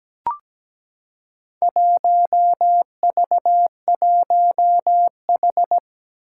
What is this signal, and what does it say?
Morse code, heard as a keyed tone, about 17 wpm, 700 Hz. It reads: 1V1H
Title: Four Challenging Letter-Number Combinations 17wpm